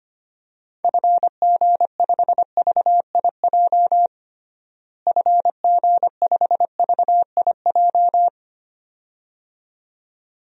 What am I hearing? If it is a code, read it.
Morse code, heard as a keyed tone, about 25 wpm, 700 Hz. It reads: FG5VIJ FG5VIJ
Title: DXCC-100 Call Signs - 6 Characters - Repeated 2x - 25wpm